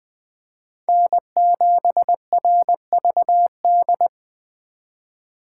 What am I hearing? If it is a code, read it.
Morse code, heard as a keyed tone, about 21 wpm, 700 Hz. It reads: N7RVD
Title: K1USN SST QSO Practice - 20wpm